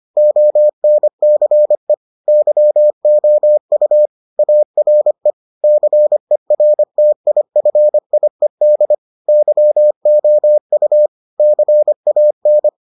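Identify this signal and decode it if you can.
Morse code, heard as a keyed tone, about 25 wpm, 600 Hz. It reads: ONCE YOU ARE CERTIFIED YOU CAN
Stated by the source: Morse edition of the Foundations of Amateur Radio episodes.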